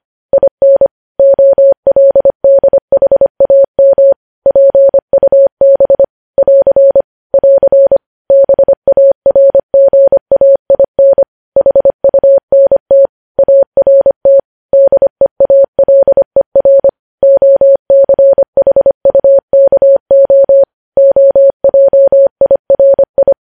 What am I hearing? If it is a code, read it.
Morse code, heard as a keyed tone, about 25 wpm, 550 Hz. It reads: IN OLDHAM PUB <AR> <AR> BARGAIN HUNT ART DEALER OCHUKO OJIRI
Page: news headlines in morse code at 25 wpm